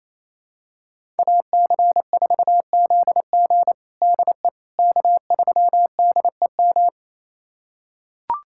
Morse code, heard as a keyed tone, about 28 wpm, 700 Hz. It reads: AC4ZG DE K3DEM
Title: USA Callsign DE Callsign - 5 Characters - 28wpm